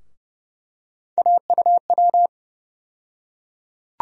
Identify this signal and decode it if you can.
Morse code, heard as a keyed tone, about 30 wpm, 700 Hz. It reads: AUW
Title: USA Call Sign Postfixes - 3 Letters 30wpm